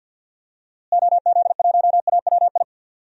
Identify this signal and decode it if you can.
Morse code, heard as a keyed tone, about 51 wpm, 700 Hz. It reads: OZ1AWI
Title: DXCC-100 Call Signs - 6 Characters - Repeated 3x - 50wpm